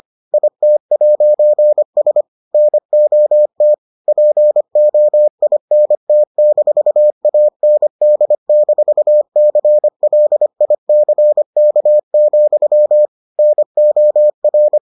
Morse code, heard as a keyed tone, about 25 wpm, 600 Hz. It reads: IT'S NOT POINT-AND-CLICK, NOR